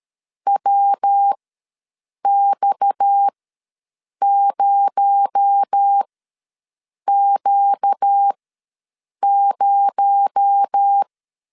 Morse code, heard as a keyed tone, about 13 wpm, spaced as for 7 wpm, 800 Hz. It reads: WX0Q0
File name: LEZIONI / 6 PKX0 / 6 PKX0 ws 12wpm 3448